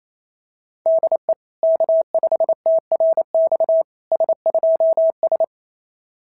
Morse code, heard as a keyed tone, about 26 wpm, 650 Hz. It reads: DE K5TRX S2S
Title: SOTA QSO Practice 28wpm